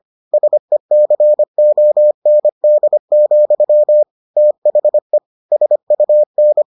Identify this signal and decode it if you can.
Morse code, heard as a keyed tone, about 25 wpm, 600 Hz. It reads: SECOND, THE SUN